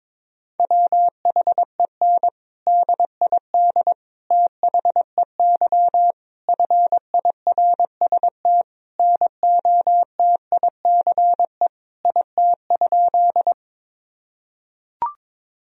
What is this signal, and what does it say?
Morse code, heard as a keyed tone, about 22 wpm, 700 Hz. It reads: WHEN DID THEY FIRST NOTICE IT?
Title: Sentences from Top 1000 Words - Review Part 2 - 22wpm